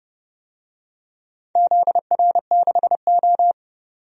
Morse code, heard as a keyed tone, about 29 wpm, 700 Hz. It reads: ZR6O
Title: DXCC-100 Call Signs - 4 Characters - 30wpm